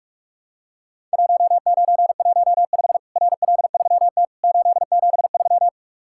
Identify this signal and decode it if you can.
Morse code, heard as a keyed tone, about 43 wpm, 700 Hz. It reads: 1915 RL3T 873
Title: Worked All Europe DX Contest - QTC Exchange - 45wpm